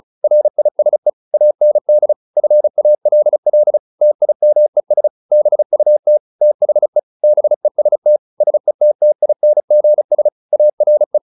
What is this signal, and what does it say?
Morse code, heard as a keyed tone, about 35 wpm, 600 Hz. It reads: RISE AND FALL TIMES BUT THE BEST SETTINGS ARE